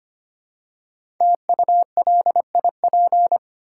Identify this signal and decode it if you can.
Morse code, heard as a keyed tone, about 25 wpm, 690 Hz. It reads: TULIP
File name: practice / 25wpm / Easy Words with Hints - 25wpm